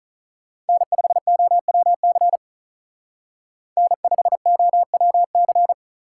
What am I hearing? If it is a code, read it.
Morse code, heard as a keyed tone, about 44 wpm, 700 Hz. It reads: N5OWC N5OWC